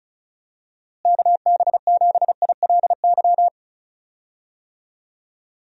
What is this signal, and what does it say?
Morse code, heard as a keyed tone, about 34 wpm, 700 Hz. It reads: KB7ILY
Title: USA Call Signs - 6 Characters - Repeated 3x - 35wpm